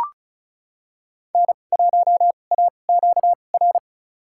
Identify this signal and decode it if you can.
Morse code, heard as a keyed tone, about 36 wpm, spaced as for 25 wpm, 700 Hz. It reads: N1AQR
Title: USA Call Signs - 5 Characters - 2x Character Spacing - 35wpm